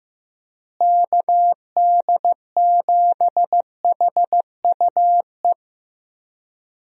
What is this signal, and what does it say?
Morse code, heard as a keyed tone, about 15 wpm, 700 Hz. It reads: KD7HUE
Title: USA Call Signs - 6 Characters - Repeated 3x - 15wpm